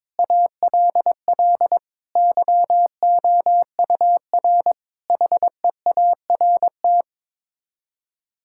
Morse code, heard as a keyed tone, about 22 wpm, 700 Hz. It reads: ALL YOUR HEART
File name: Sets of 9 Words - 22wpm